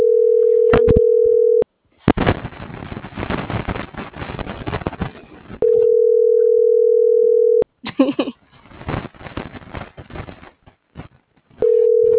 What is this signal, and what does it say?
Ringback tone (North America), 440 + 480 Hz.